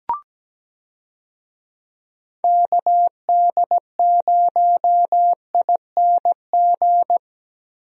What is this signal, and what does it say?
Morse code, heard as a keyed tone, about 17 wpm, 700 Hz. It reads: KD0ING
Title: USA Call Signs - 6 Characters - Repeated 2x - 17wpm